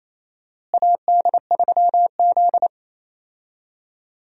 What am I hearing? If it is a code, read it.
Morse code, heard as a keyed tone, about 28 wpm, 700 Hz. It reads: AD3Z
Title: USA Call Signs - 4 Characters - Repeated 2x - 28wpm